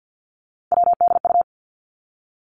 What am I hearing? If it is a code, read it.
Morse code, heard as a keyed tone, about 50 wpm, 700 Hz. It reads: WDU
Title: Three Challenging Letter-Number Combinations 50wpm